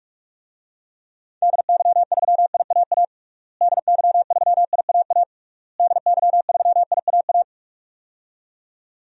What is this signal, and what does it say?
Morse code, heard as a keyed tone, about 41 wpm, 700 Hz. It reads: DY3IAA DY3IAA DY3IAA
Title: DXCC-100 Call Signs - 6 Characters - Repeated 3x - 45wpm